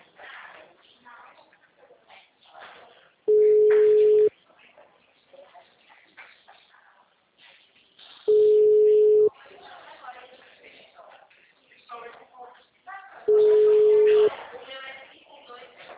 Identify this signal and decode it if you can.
Ringback tone (Europe, 425 Hz), with speech.
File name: out- -2016- - - .33911